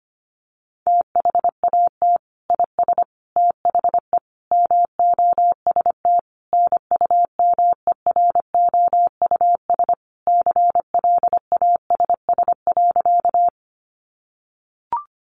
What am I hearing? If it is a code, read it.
Morse code, heard as a keyed tone, about 25 wpm, 700 Hz. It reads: THAT IS THE MOST NUMEROUS CLASS.
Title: Sentences from Top 1400 Words 25wpm